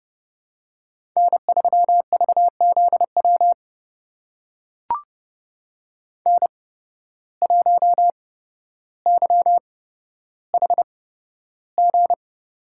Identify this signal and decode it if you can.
Morse code, heard as a keyed tone, about 30 wpm, spaced as for 13 wpm, 700 Hz. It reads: N3VZW N1YHG
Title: USA Call Signs - 5 Characters - 8x Character Spacing - 30wpm